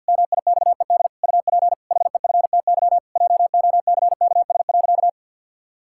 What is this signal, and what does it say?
Morse code, heard as a keyed tone, about 50 wpm, 700 Hz. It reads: MIXED UP HEFTY JOCKS.
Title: Pangram Challenge v2 - 50wpm